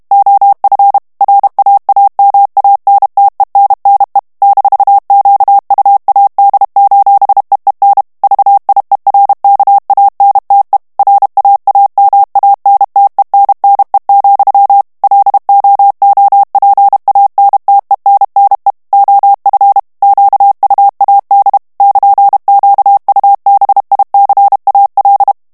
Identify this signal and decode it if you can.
Morse code, heard as a keyed tone, about 32 wpm, 800 Hz. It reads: OF RAAMANTENNE -QUAD:EEN VIERKANTE RAAMANTENNE, LOOPANTENNE OF QUAD <KN>QUBICAL
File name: morse mp3 / HOLLANDAIS texte clair 1 vitesse 30 mots minute